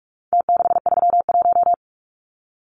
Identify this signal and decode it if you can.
Morse code, heard as a keyed tone, about 44 wpm, 700 Hz. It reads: T631
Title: Numbers - Contest Style - 4-Digit 45wpm